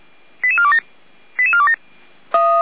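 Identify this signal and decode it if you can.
Five-tone selective call (ZVEI), heard as quick runs of five tones, tones of about 70 ms. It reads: 88427 88427
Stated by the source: ZVEI five-tone selective call received by SDR, Feuerwehr Orlamünde fire service.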